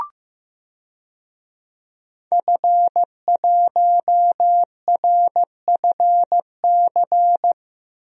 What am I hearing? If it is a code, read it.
Morse code, heard as a keyed tone, about 15 wpm, 700 Hz. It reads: F1RFC
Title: DXCC-100 Call Signs - 5 Characters - Repeated 3x - 15wpm